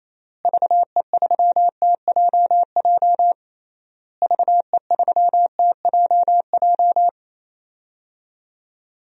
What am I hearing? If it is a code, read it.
Morse code, heard as a keyed tone, about 28 wpm, 700 Hz. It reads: VE3TJJ VE3TJJ
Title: DXCC-100 Call Signs - 6 Characters - Repeated 3x - 28wpm